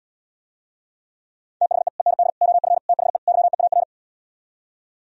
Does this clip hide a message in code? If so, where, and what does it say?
Yes: Morse code from 1.6 to 3.8 s, heard as a keyed tone, about 77 wpm, 700 Hz. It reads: THE END OF THE YEAR